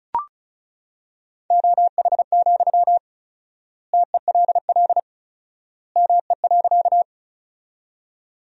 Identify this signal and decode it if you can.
Morse code, heard as a keyed tone, about 36 wpm, 700 Hz. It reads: OH, TELL ME.
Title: 4x Word-Spacing - Sentences from Top 500 Words - 35wpm